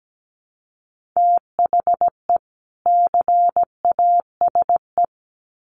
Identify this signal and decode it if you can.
Morse code, heard as a keyed tone, about 17 wpm, 700 Hz. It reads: THE CASE
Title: Sets of 2 Words 17wpm